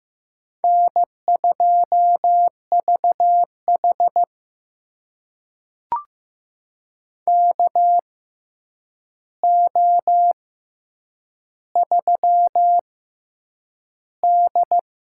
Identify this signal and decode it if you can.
Morse code, heard as a keyed tone, about 15 wpm, spaced as for 8 wpm, 700 Hz. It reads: N2VH KO3D
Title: USA Call Signs - 4 Characters - 6x Character Spacing - 15wpm